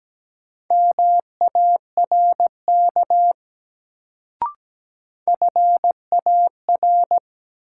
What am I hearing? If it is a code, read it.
Morse code, heard as a keyed tone, about 17 wpm, 700 Hz. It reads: MARK FAR
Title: Sentences from Top 300 Words - New Word Drill - 17wpm